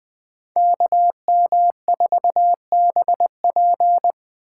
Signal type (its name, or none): Morse code